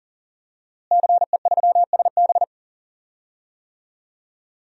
Morse code, heard as a keyed tone, about 40 wpm, 700 Hz. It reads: CE3SB
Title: DXCC-100 Call Signs - 5 Characters - Repeated 2x - 40wpm